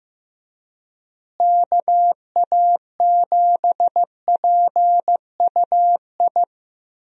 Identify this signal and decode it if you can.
Morse code, heard as a keyed tone, about 15 wpm, 700 Hz. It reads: KA7PUI
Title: USA Call Signs - 6 Characters - Repeated 3x - 15wpm